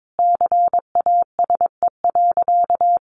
Morse code, heard as a keyed tone, about 22 wpm, 700 Hz. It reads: CASE.
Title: Sentences from Top 1400 Words 22wpm